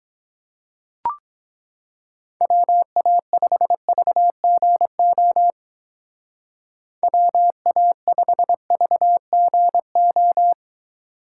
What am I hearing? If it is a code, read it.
Morse code, heard as a keyed tone, about 20 wpm, 700 Hz. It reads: WA5VGO WA5VGO
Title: Speed Racing - USA 2x3 Call Signs 17wpm